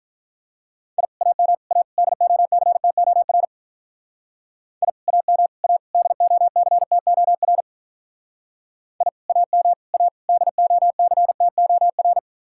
Morse code, heard as a keyed tone, about 43 wpm, 700 Hz. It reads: I AM A DOCTOR I AM A DOCTOR I AM A DOCTOR
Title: Speed-Racing Common QSO Phrases Encore - 35wpm